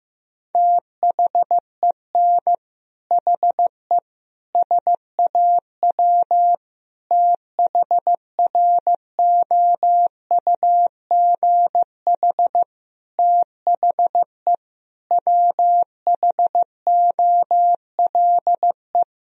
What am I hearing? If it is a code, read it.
Morse code, heard as a keyed tone, about 15 wpm, 700 Hz. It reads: THEN HE SAW THROUGH THE WHOLE
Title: Sentences from Top 1000 Words - Review Part 1 - 15wpm